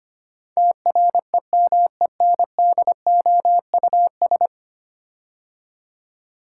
Morse code, heard as a keyed tone, about 25 wpm, 700 Hz. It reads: TREMENDOUS
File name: 10 Letter Words - First 3 Letters Given - 25wpm